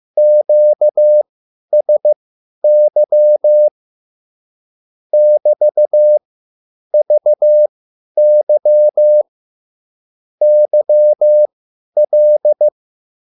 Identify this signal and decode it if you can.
Morse code, heard as a keyed tone, about 15 wpm, spaced as for 10 wpm, 600 Hz. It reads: QSY <BT>VY YL
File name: QSL / CW lesson